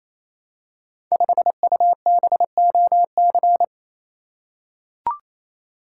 Morse code, heard as a keyed tone, about 29 wpm, 700 Hz. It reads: 5UBOC
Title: Code Groups - 5 Letter:Numbers - 28wpm